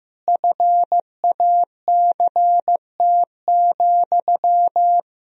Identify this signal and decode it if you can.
Morse code, heard as a keyed tone, about 15 wpm, 700 Hz. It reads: FACT,
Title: Sentences from Top 1000 Words - Review Part 3 - 15wpm